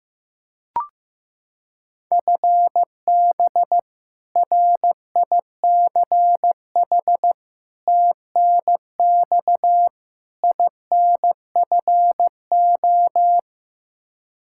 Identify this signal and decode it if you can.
Morse code, heard as a keyed tone, about 15 wpm, 700 Hz. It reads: FB RICH TNX INFO
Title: Practice QSO - Exchange 3 - More Info - 15wpm